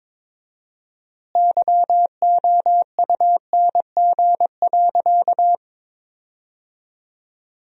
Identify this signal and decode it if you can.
Morse code, heard as a keyed tone, about 22 wpm, 700 Hz. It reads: YOUNG.
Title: 4x Word-Spacing - Sentences from Top 400 Words - 22wpm